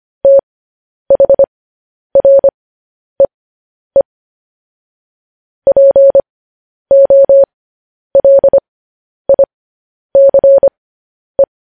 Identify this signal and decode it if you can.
Morse code, heard as a keyed tone, about 25 wpm, spaced as for 10 wpm, 550 Hz. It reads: THREE POLICE
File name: podcast / News headlines-1x-25x10